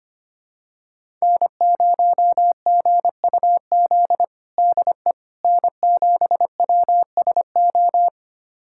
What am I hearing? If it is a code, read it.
Morse code, heard as a keyed tone, about 25 wpm, 700 Hz. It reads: N0GUZ DE N7WSO
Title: USA Callsign DE Callsign - 5 Characters - 25wpm